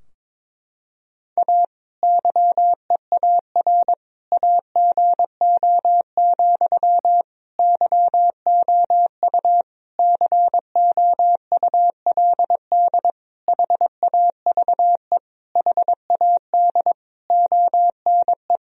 Morse code, heard as a keyed tone, about 22 wpm, 700 Hz. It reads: A YEAR AGO, YOU COULD HAVE HAD ONE
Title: Sentences from Top 1000 Words - Review Part 3 - 22wpm